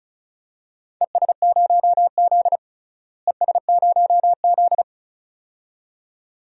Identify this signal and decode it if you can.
Morse code, heard as a keyed tone, about 36 wpm, 700 Hz. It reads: ES0Z ES0Z
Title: DXCC-100 Call Signs - 4 Characters - Repeated 2x - 35wpm